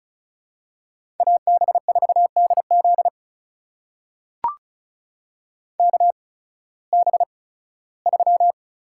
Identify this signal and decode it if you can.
Morse code, heard as a keyed tone, about 34 wpm, spaced as for 18 wpm, 700 Hz. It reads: AB4DZ KB3
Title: USA Call Signs - 5 Characters - 8x Character Spacing - 35wpm